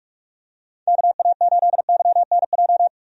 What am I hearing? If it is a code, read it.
Morse code, heard as a keyed tone, about 46 wpm, 700 Hz. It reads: KA8YNJ
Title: USA Call Signs - 6 Characters - 45wpm